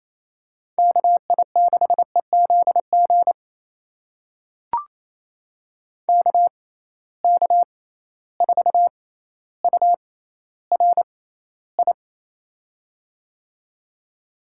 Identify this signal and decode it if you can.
Morse code, heard as a keyed tone, about 28 wpm, spaced as for 14 wpm, 700 Hz. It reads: KI6EZG KK4URI
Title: USA Call Signs - 6 Characters - 6x Character Spacing - 28wpm